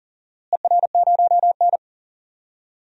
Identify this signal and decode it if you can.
Morse code, heard as a keyed tone, about 41 wpm, 700 Hz. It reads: ER0N
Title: DXCC-100 Call Signs - 4 Characters - Repeated 2x - 40wpm